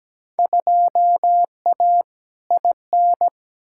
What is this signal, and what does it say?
Morse code, heard as a keyed tone, about 17 wpm, 700 Hz. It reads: 2A IN
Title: Field Day Exchanges - 17wpm